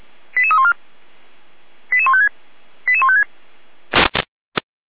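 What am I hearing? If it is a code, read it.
Five-tone selective call (ZVEI), heard as quick runs of five tones, tones of about 70 ms. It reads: 88314 88156 88156
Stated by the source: ZVEI five-tone selective call received by SDR.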